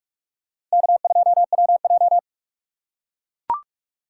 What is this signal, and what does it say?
Morse code, heard as a keyed tone, about 42 wpm, 700 Hz. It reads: K2WJ
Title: USA Call Signs - 4 Characters - 3x Character Spacing - 45wpm